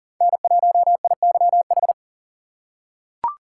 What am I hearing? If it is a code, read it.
Morse code, heard as a keyed tone, about 38 wpm, 700 Hz. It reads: N1IYH
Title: USA Call Signs - 5 Characters - 6x Character Spacing - 40wpm